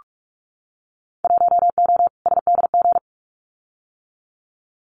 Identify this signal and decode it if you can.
Morse code, heard as a keyed tone, about 46 wpm, 700 Hz. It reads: 1O SDG
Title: Winter Field Day Exchanges Encore 45wpm